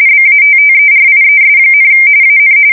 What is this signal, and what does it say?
RTTY, about 45.45 baud. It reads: BATTERIES, AN
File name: fsk170 / USS-Massachusetts